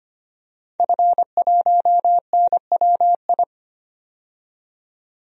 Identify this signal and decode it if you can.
Morse code, heard as a keyed tone, about 25 wpm, 700 Hz. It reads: F1NWI
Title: DXCC-100 Call Signs - 5 Characters - 25wpm